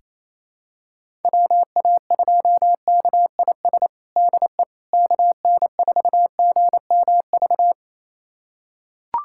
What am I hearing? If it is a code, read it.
Morse code, heard as a keyed tone, about 28 wpm, 700 Hz. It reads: WA2KIS DE KN4GMV
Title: USA Callsign DE Callsign 28wpm